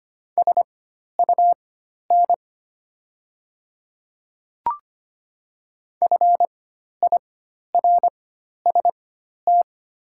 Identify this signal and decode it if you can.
Morse code, heard as a keyed tone, about 25 wpm, spaced as for 12 wpm, 700 Hz. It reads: SUN FIRST
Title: 4x Character Spacing - Top 300 Words - 25wpm